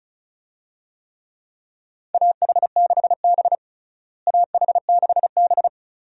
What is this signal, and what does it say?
Morse code, heard as a keyed tone, about 36 wpm, 700 Hz. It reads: AH6B AH6B
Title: USA Call Signs - 4 Characters - Repeated 2x - 35wpm